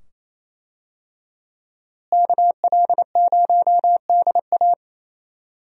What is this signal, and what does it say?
Morse code, heard as a keyed tone, about 28 wpm, 700 Hz. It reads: KL0DA
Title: USA Call Signs - 5 Characters - Repeated 2x - 28wpm